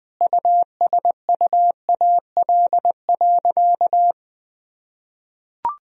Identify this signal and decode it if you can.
Morse code, heard as a keyed tone, about 20 wpm, 700 Hz. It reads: USUAL.
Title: Sentences from Top 1000 Words - Review Part 2 - 20wpm